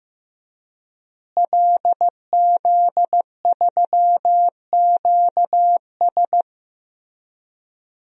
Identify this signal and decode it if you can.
Morse code, heard as a keyed tone, about 15 wpm, 700 Hz. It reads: LZ3QS
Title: DXCC-100 Call Signs - 5 Characters - Repeated 3x - 15wpm